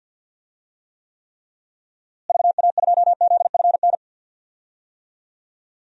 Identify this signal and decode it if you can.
Morse code, heard as a keyed tone, about 48 wpm, 700 Hz. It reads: VA2ZFN
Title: DXCC-100 Call Signs - 6 Characters - Repeated 3x - 50wpm